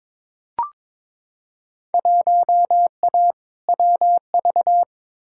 Morse code, heard as a keyed tone, about 21 wpm, 700 Hz. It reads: 1A WV
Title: Field Day Exchanges Encore - 22wpm